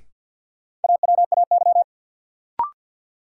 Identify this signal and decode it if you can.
Morse code, heard as a keyed tone, about 51 wpm, 700 Hz. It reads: AWAY
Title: Sets of 2 Words - New Word Drill - 50wpm